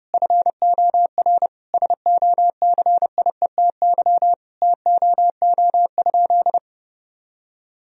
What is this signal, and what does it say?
Morse code, heard as a keyed tone, about 30 wpm, 700 Hz. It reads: FOR SOCIETY TOO?
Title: Sentences from Top 1000 Words - Review Part 4 - 30wpm